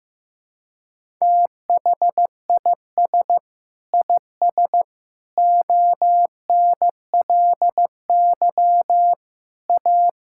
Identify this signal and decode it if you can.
Morse code, heard as a keyed tone, about 15 wpm, 700 Hz. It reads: THIS IS ONLY A
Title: Sentences from Top 1000 Words - Review Part 5 - 15wpm